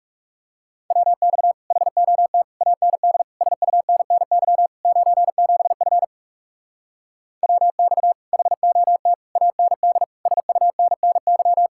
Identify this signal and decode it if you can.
Morse code, heard as a keyed tone, about 40 wpm, 700 Hz. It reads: WX HOT AND SUNNY 97F WX HOT AND SUNNY